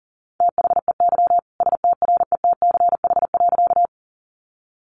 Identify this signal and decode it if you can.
Morse code, heard as a keyed tone, about 40 wpm, 700 Hz. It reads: THEY STRETCH.
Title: Easy Phrases 40wpm